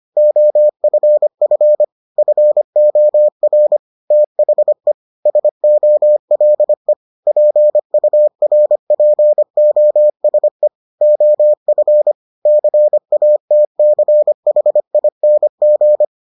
Morse code, heard as a keyed tone, about 25 wpm, 600 Hz. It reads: OFF FOR THE SOLE PURPOSE OF CATCHING